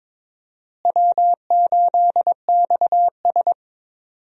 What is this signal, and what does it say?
Morse code, heard as a keyed tone, about 22 wpm, 700 Hz. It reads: W8XS
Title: USA Call Signs - 4 Characters - Repeated 3x - 22wpm